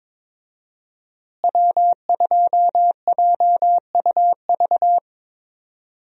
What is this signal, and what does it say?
Morse code, heard as a keyed tone, about 22 wpm, 700 Hz. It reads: W2JUV
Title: USA Call Signs - 5 Characters - Repeated 2x - 22wpm